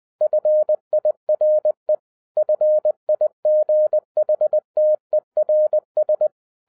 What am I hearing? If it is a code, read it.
Morse code, heard as a keyed tone, about 20 wpm, 610 Hz. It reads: FIRE FIGHTERS